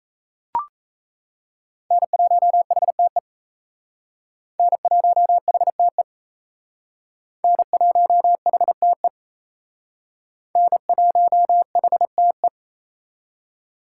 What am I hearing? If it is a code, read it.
Morse code, heard as a keyed tone, about 36 wpm, 700 Hz. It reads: N1HTE N1HTE N1HTE N1HTE